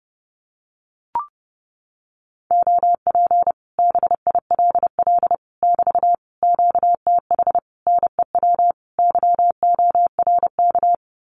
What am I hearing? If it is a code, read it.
Morse code, heard as a keyed tone, about 30 wpm, 700 Hz. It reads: OP BILL <BT> QTH NEW YORK
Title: Common QSO Phrases Encore 30wpm